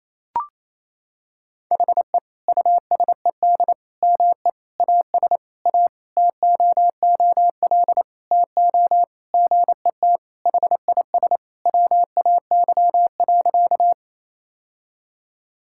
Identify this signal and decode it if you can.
Morse code, heard as a keyed tone, about 28 wpm, 700 Hz. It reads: HE USED ME AS A TOOL TO GET HIS WAY.
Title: Sentences from Top 1000 Words - Review Part 6 - 28wpm